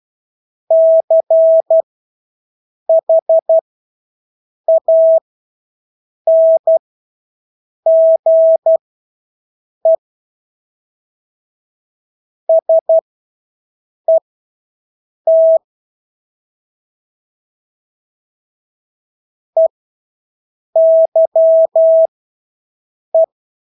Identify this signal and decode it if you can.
Morse code, heard as a keyed tone, about 12 wpm, spaced as for 6 wpm, 650 Hz. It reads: CHANGE SET EYE
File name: Engelske ord 12-6 wpm